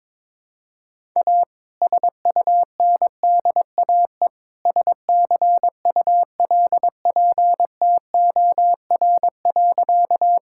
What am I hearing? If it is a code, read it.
Morse code, heard as a keyed tone, about 21 wpm, 700 Hz. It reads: A SUNDAE SCULPTOR.
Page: morse-code-ninja-practice-22wpm